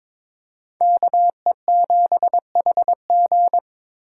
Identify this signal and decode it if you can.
Morse code, heard as a keyed tone, about 26 wpm, 700 Hz. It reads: KE7HG
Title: Speed Racing - USA 2x2 Call Signs 22wpm